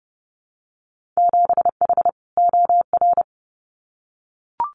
Morse code, heard as a keyed tone, about 30 wpm, 700 Hz. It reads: 7H OR
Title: Winter Field Day Exchanges Encore 30wpm